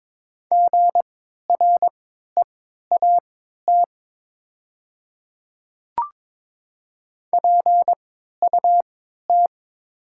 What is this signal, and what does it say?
Morse code, heard as a keyed tone, about 21 wpm, spaced as for 12 wpm, 700 Hz. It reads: GREAT PUT
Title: 3x Character Spacing - Top 200 Words - 22wpm